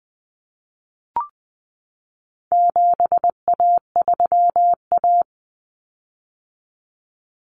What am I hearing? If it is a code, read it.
Morse code, heard as a keyed tone, about 20 wpm, 700 Hz. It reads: 7A3A
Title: Common Cut Numbers - 4-Digit 20wpm